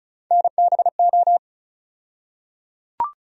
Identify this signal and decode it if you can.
Morse code, heard as a keyed tone, about 36 wpm, 700 Hz. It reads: NBO
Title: USA Call Sign Postfixes - 3 Letters 35wpm